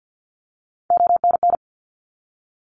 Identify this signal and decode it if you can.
Morse code, heard as a keyed tone, about 50 wpm, 700 Hz. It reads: ONN
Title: ARRL Sections 50wpm